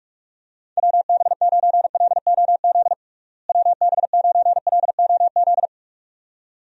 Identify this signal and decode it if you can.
Morse code, heard as a keyed tone, about 46 wpm, 700 Hz. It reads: WB9LOZ WB9LOZ
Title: USA Call Signs - 6 Characters - Repeated 2x - 45wpm